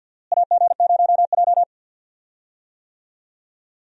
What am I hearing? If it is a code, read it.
Morse code, heard as a keyed tone, about 52 wpm, 700 Hz. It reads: AG0J
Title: USA Call Signs - 4 Characters - Repeated 2x - 50wpm